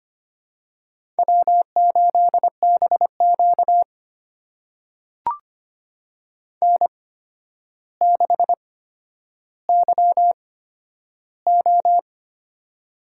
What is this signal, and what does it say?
Morse code, heard as a keyed tone, about 25 wpm, spaced as for 11 wpm, 700 Hz. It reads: W8BQ N6YO
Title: USA Call Signs - 4 Characters - 8x Character Spacing - 25wpm